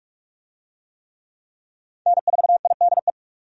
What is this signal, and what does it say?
Morse code, heard as a keyed tone, about 47 wpm, 700 Hz. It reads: N4IDE
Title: USA Call Signs - 5 Characters - 45wpm